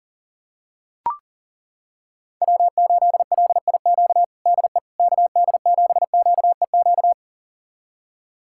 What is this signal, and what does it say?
Morse code, heard as a keyed tone, about 40 wpm, 700 Hz. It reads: W8LIQ DE KD7QEQ
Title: USA Callsign DE Callsign 40wpm